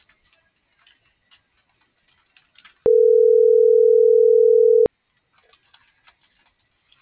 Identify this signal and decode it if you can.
Ringback tone (North America), 440 + 480 Hz.